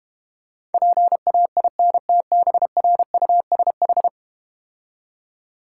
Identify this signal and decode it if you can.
Morse code, heard as a keyed tone, about 32 wpm, 700 Hz. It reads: PAINTBRUSH